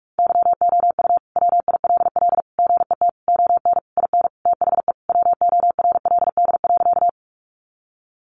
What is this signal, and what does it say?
Morse code, heard as a keyed tone, about 44 wpm, 700 Hz. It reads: YOU WILL GET ON IN THE WORLD.
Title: Sentences from Top 200 Words 45wpm